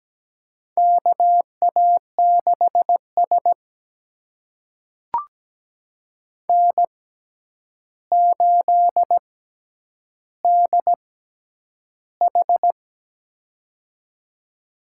Morse code, heard as a keyed tone, about 17 wpm, spaced as for 9 wpm, 700 Hz. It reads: KA6S N8DH
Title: USA Call Signs - 4 Characters - 6x Character Spacing - 17wpm